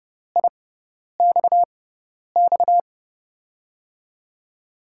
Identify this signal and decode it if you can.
Morse code, heard as a keyed tone, about 30 wpm, spaced as for 10 wpm, 700 Hz. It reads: IXX
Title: USA Call Signs - 6 Characters - 6x Character Spacing - 30wpm